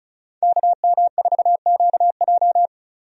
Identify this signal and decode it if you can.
Morse code, heard as a keyed tone, about 36 wpm, 700 Hz. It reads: KM4QJ
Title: USA Call Signs - 5 Characters - Repeated 2x - 35wpm